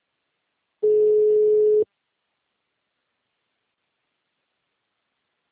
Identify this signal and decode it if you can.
Ringback tone (Europe, 425 Hz).